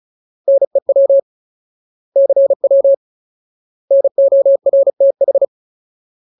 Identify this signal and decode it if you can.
Morse code, heard as a keyed tone, about 34 wpm, 550 Hz. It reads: NEW CW NORTH
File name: Ultimate Word List / 35WPM Ultimate